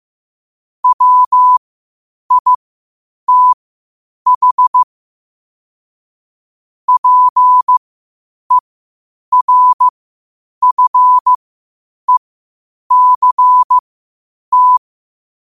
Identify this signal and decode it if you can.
Morse code, heard as a keyed tone, about 15 wpm, spaced as for 8 wpm, 1000 Hz. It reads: WITH PERFECT
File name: qotdfiles / 8wpm